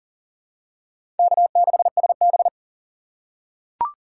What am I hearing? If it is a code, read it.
Morse code, heard as a keyed tone, about 40 wpm, 700 Hz. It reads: K6SB
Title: USA Call Signs - 6x Character Spacing - 40wpm